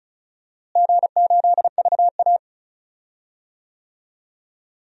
Morse code, heard as a keyed tone, about 36 wpm, 700 Hz. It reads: G8VA
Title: DXCC-100 Call Signs - 4 Characters - Repeated 3x - 35wpm